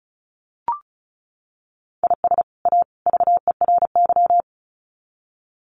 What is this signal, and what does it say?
Morse code, heard as a keyed tone, about 34 wpm, 700 Hz. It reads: IS A VERY
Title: Sets of 3 Words 35wpm